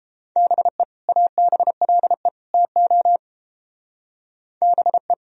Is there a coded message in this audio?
Yes: Morse code from 0.4 to 5.1 s, heard as a keyed tone, about 29 wpm, 700 Hz. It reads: BE ABLE TO BE